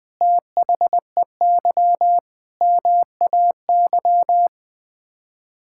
Morse code, heard as a keyed tone, about 20 wpm, 700 Hz. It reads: THEY MAY
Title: Sets of 8 Words - 20wpm